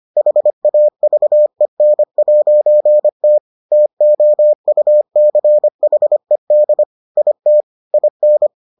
Morse code, heard as a keyed tone, about 25 wpm, 600 Hz. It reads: HAVEN'T TOUCHED IT IN